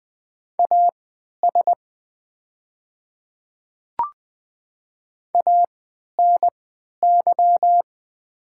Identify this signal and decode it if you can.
Morse code, heard as a keyed tone, about 20 wpm, spaced as for 11 wpm, 700 Hz. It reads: AS ANY
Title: 3x Character Spacing - Top 200 Words - 20wpm